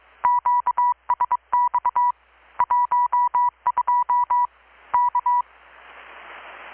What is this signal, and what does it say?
Morse code, heard as a keyed tone, about 23 wpm, 980 Hz. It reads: QSX 12 K
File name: blog / swl / 6-abril-2019-12695-USB-KPS